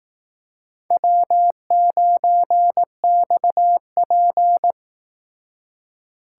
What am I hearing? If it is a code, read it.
Morse code, heard as a keyed tone, about 18 wpm, 700 Hz. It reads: W9XP
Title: Speed Racing - USA Call Signs 15wpm